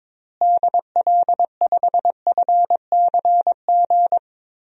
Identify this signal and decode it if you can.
Morse code, heard as a keyed tone, about 22 wpm, 700 Hz. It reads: DL5FCG
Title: DXCC-100 Call Signs - 6 Characters - Repeated 2x - 22wpm